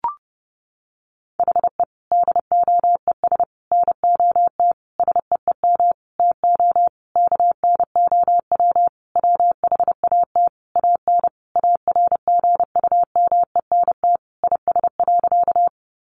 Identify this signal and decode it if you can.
Morse code, heard as a keyed tone, about 30 wpm, 700 Hz. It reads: HE DOES NOT SEEM TO KNOW WHAT AN ARGUMENT IS.
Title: Sentences from Top 1400 Words 30wpm